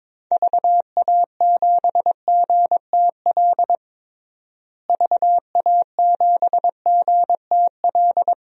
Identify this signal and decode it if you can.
Morse code, heard as a keyed tone, about 22 wpm, 700 Hz. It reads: VA7GTL VA7GTL
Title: DXCC-100 Call Signs - 6 Characters - Repeated 3x - 22wpm